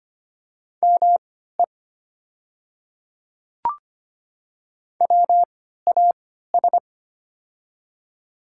Morse code, heard as a keyed tone, about 25 wpm, spaced as for 14 wpm, 700 Hz. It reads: ME WAS
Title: 3x Character Spacing - Top 200 Words - 25wpm